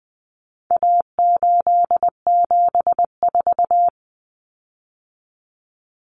Morse code, heard as a keyed tone, about 20 wpm, 700 Hz. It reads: A874
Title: Common Cut Numbers - 4-Digit 20wpm